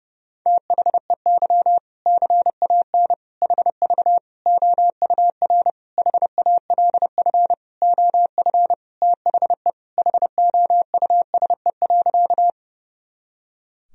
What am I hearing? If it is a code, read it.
Morse code, heard as a keyed tone, about 32 wpm, 700 Hz. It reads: THEY CAN HV OUR HALF OF THE HOUSE.
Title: Sentences from Top 400 Words and Abbreviations 30wpm